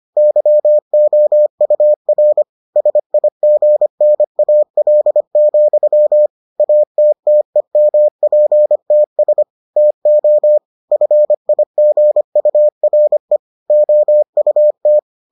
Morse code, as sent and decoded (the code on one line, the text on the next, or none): -.-- --- ..- .-. / ... .. --. -. .- .-.. --..-- / .- - - . -- .--. - ... / - --- / ..-. .. --. ..- .-. / --- ..- -
YOUR SIGNAL, ATTEMPTS TO FIGURE OUT